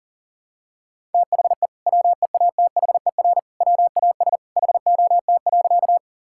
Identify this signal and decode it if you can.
Morse code, heard as a keyed tone, about 40 wpm, 700 Hz. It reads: THE WEATHER WAS HOT.
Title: Sentences from Top 1000 Words - Review Part 5 - 40wpm